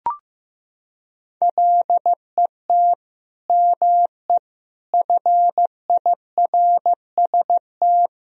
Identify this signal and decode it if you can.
Morse code, heard as a keyed tone, about 16 wpm, 700 Hz. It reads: LET ME FIRST
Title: Sentences from Top 1000 Words - Review Part 2 - 15wpm